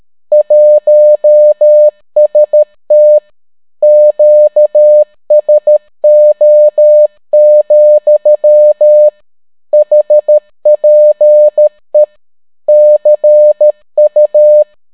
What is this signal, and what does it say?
Morse code, heard as a keyed tone, about 13 wpm, 600 Hz. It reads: 1ST QSO, HPE CU